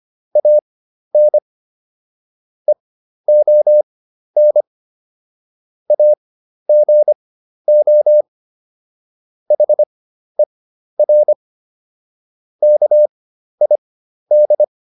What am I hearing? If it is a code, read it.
Morse code, heard as a keyed tone, about 25 wpm, spaced as for 12 wpm, 600 Hz. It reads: AN EON AGO HER KID